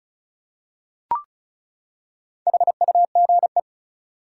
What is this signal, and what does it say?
Morse code, heard as a keyed tone, about 34 wpm, 700 Hz. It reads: HUGE
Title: Four Letter Words - 35wpm